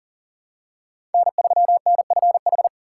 Morse code, heard as a keyed tone, about 40 wpm, 690 Hz. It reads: N3NFH
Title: USA Call Signs - 4x Character Spacing - 40wpm